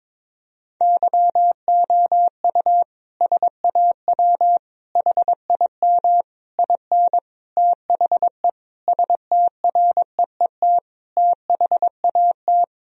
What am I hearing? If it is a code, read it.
Morse code, heard as a keyed tone, about 22 wpm, 700 Hz. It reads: YOU SAW HIM IN THE STREET THAT
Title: Sentences from Top 1000 Words - Review Part 3 - 22wpm